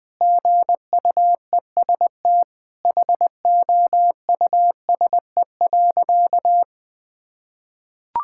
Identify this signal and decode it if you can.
Morse code, heard as a keyed tone, about 21 wpm, 700 Hz. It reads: GUEST HOUSE.
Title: Sentences from Top 1600 Words 20wpm